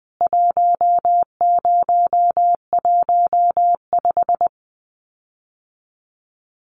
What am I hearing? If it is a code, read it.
Morse code, heard as a keyed tone, about 20 wpm, 700 Hz. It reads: 1015
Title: CWT Exchange 20wpm